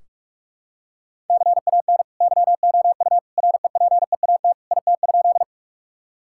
Morse code, heard as a keyed tone, about 41 wpm, 700 Hz. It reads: CAN YOU REPEAT IT?
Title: Sentences from Top 1600 Words 45wpm